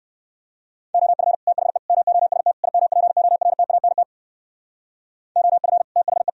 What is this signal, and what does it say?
Morse code, heard as a keyed tone, about 55 wpm, 700 Hz. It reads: OF THE MOST IMPORTANT OF THE
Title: Speed-Racing - Sets of 4 Words 45wpm